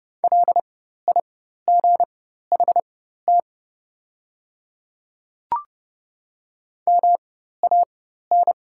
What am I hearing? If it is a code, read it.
Morse code, heard as a keyed tone, about 30 wpm, spaced as for 14 wpm, 700 Hz. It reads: LIGHT MAN
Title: 4x Character Spacing - Top 200 Words - 30wpm